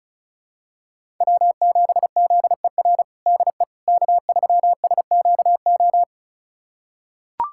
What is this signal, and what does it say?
Morse code, heard as a keyed tone, about 34 wpm, 700 Hz. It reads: W7ZER DE K3SQO
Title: USA Callsign DE Callsign - 5 Characters - 35wpm